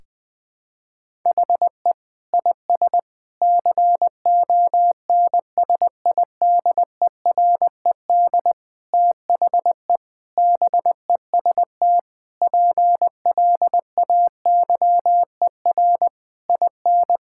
Morse code, heard as a keyed tone, about 19 wpm, 700 Hz. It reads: HE IS CONSIDERED THE BEST PLAYER IN
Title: Sentences with 6-Word Phrases - Spoken Beginning - 20wpm